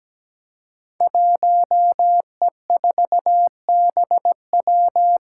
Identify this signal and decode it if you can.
Morse code, heard as a keyed tone, about 17 wpm, 700 Hz. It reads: 1E4BW
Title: Five Challenging Letter-Number Combinations 17wpm